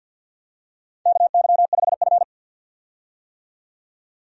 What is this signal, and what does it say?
Morse code, heard as a keyed tone, about 52 wpm, 700 Hz. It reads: KY5F
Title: USA Call Signs - 4 Characters - 50wpm